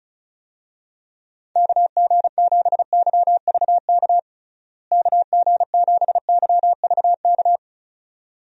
Morse code, heard as a keyed tone, about 36 wpm, 700 Hz. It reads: KG7YVK KG7YVK
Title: USA Call Signs - 6 Characters - Repeated 2x - 35wpm